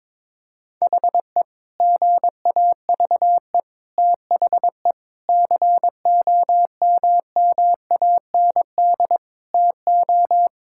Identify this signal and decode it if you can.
Morse code, heard as a keyed tone, about 22 wpm, 700 Hz. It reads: HE GAVE THE COMMAND TO
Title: Sentences from Top 1600 Words 22wpm